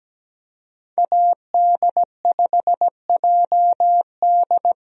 Morse code, heard as a keyed tone, about 17 wpm, 700 Hz. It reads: AD5JD
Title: USA Call Signs - 5 Characters - Repeated 2x - 17wpm